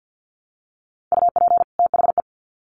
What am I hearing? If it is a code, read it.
Morse code, heard as a keyed tone, about 50 wpm, 700 Hz. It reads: UP THE
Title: Sets of 2 Words 50wpm